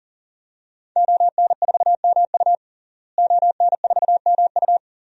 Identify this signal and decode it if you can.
Morse code, heard as a keyed tone, about 37 wpm, 700 Hz. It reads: ON4MU ON4MU
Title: DXCC-100 Call Signs - 5 Characters - Repeated 3x - 40wpm